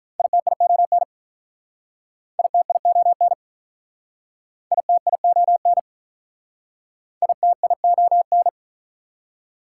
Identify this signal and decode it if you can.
Morse code, heard as a keyed tone, about 46 wpm, 690 Hz. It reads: ITION ITION ITION ITION